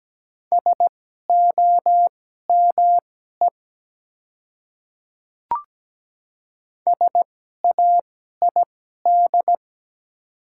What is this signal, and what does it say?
Morse code, heard as a keyed tone, about 17 wpm, spaced as for 12 wpm, 690 Hz. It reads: SOME SAID
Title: 2x Character Spacing - Top 100 Words - 17wpm